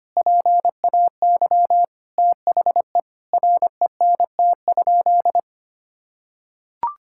Morse code, heard as a keyed tone, about 25 wpm, 700 Hz. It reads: PAY THE RENT?
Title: Sentences from Top 1000 Words - Review Part 5 - 25wpm